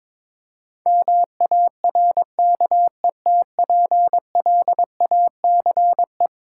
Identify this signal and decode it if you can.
Morse code, heard as a keyed tone, about 21 wpm, 700 Hz. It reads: MARKETPLACE
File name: practice / 22wpm / Eleven Letter Words - 22wpm